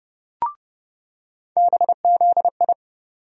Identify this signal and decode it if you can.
Morse code, heard as a keyed tone, about 30 wpm, 700 Hz. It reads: BZI
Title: USA Call Sign Postfixes - 3 Letters 30wpm